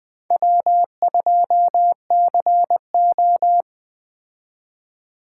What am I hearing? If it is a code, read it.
Morse code, heard as a keyed tone, about 20 wpm, 700 Hz. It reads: W2CO